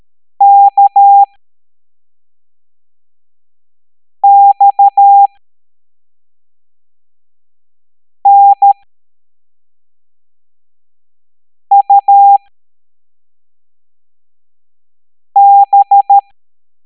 Morse code, heard as a keyed tone, about 13 wpm, spaced as for 3 wpm, 800 Hz. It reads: KXNUB